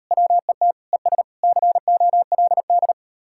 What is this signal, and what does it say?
Morse code, heard as a keyed tone, about 39 wpm, 700 Hz. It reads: WET ES COLD